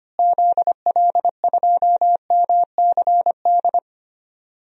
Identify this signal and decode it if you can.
Morse code, heard as a keyed tone, about 25 wpm, 700 Hz. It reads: ZL2MCD
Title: DXCC-100 Call Signs - 6 Characters - Repeated 2x - 25wpm